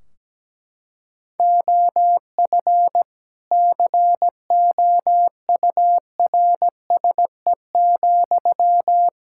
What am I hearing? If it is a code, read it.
Morse code, heard as a keyed tone, about 17 wpm, 700 Hz. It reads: OF COURSE,
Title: Sentences from Top 1600 Words 17wpm